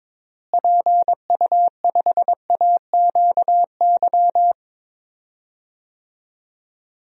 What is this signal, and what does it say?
Morse code, heard as a keyed tone, about 22 wpm, 700 Hz. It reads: PU5AQY
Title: DXCC-100 Call Signs - 6 Characters - Repeated 2x - 22wpm